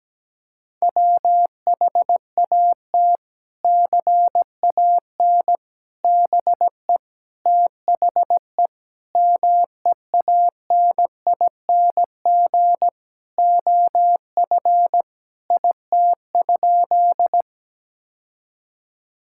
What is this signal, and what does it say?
Morse code, heard as a keyed tone, about 17 wpm, 700 Hz. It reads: WHAT CAN BE THE MEANING OF IT?
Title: Sentences from Top 1000 Words - Review Part 6 - 17wpm